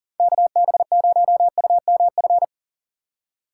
Morse code, heard as a keyed tone, about 39 wpm, 700 Hz. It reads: KB0UMF
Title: USA Call Signs - 6 Characters - 3x Character Spacing - 40wpm